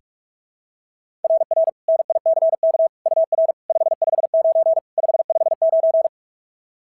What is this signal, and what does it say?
Morse code, heard as a keyed tone, about 42 wpm, 690 Hz. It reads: RR NICK UR 559 559
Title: SOTA QSO Practice 45wpm